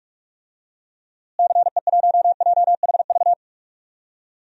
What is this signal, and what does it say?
Morse code, heard as a keyed tone, about 46 wpm, 700 Hz. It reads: CE1JHV
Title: DXCC-100 Call Signs - 6 Characters - Repeated 3x - 45wpm